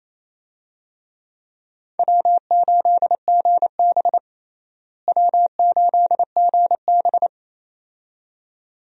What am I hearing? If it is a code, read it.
Morse code, heard as a keyed tone, about 27 wpm, 700 Hz. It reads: W8GB W8GB
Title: USA Call Signs - 4 Characters - Repeated 2x - 28wpm